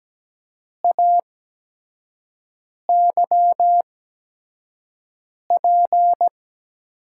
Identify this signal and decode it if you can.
Morse code, heard as a keyed tone, about 17 wpm, spaced as for 5 wpm, 700 Hz. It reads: AYP
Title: USA Call Signs - 6 Characters - 8x Character Spacing - 17wpm